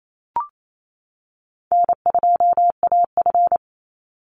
Morse code, heard as a keyed tone, about 28 wpm, 700 Hz. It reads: N2AF
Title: NA SOTA Chaser Callsigns 28wpm